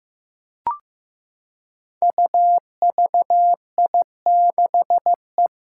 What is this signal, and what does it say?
Morse code, heard as a keyed tone, about 15 wpm, 700 Hz. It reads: UVI6E
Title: Code Groups - 5 Letter:Numbers - 15wpm